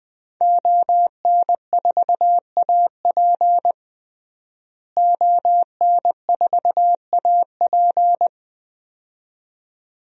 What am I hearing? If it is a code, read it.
Morse code, heard as a keyed tone, about 20 wpm, 700 Hz. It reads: ON4AP ON4AP
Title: DXCC-100 Call Signs - 5 Characters - Repeated 2x - 20wpm